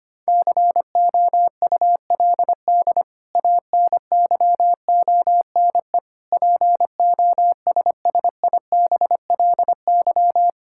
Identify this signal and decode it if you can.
Morse code, heard as a keyed tone, about 25 wpm, 700 Hz. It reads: COULD ANYONE POSSIBLY
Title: Sentences from Top 1600 Words 25wpm